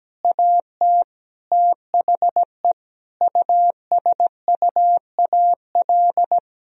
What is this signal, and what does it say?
Morse code, heard as a keyed tone, about 17 wpm, 700 Hz. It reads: AT THE USUAL
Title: Sentences from Top 1000 Words - Review Part 2 - 17wpm